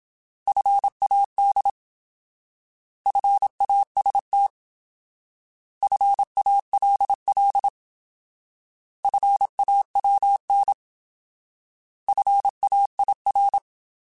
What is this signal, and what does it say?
Morse code, heard as a keyed tone, about 27 wpm, 800 Hz. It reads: FAD FAST FALL FAWN FAIR
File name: Alphabetical / Alphabetical mp3s / Fs @25